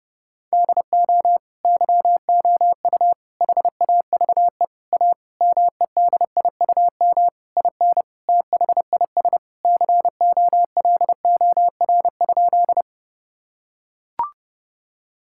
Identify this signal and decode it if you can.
Morse code, heard as a keyed tone, about 30 wpm, 700 Hz. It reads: DO YOU HAVE A MEDIUM IN THIS COLOR?
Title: Sentences from Top 1000 Words - Review Part 6 - 30wpm